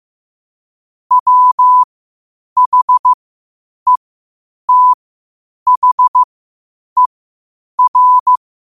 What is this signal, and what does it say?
Morse code, heard as a keyed tone, about 15 wpm, spaced as for 8 wpm, 1000 Hz. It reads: WHETHER
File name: qotdfiles / 8wpm